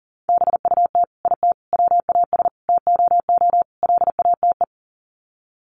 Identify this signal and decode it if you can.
Morse code, heard as a keyed tone, about 37 wpm, 700 Hz. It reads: BUT IT WAS TOO LATE
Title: Sets of 5 Words Encore2 40wpm